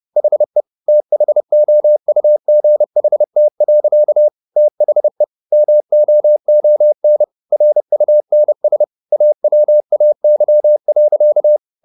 Morse code, heard as a keyed tone, about 30 wpm, 600 Hz. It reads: HE THOUGHT. THE MOON RUNS AWAY.